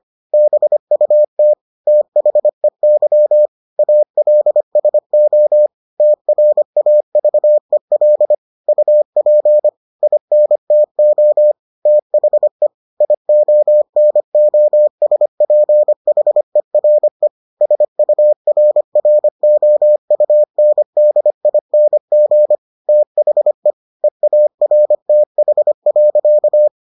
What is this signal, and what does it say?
Morse code, heard as a keyed tone, about 25 wpm, 600 Hz. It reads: BUT THEY ALSO TRAVEL UP INTO THE IONOSPHERE SURROUNDING THE EARTH.